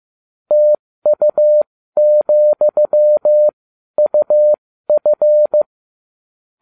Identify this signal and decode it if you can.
Morse code, heard as a keyed tone, about 15 wpm, spaced as for 12 wpm, 600 Hz. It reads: TU,UF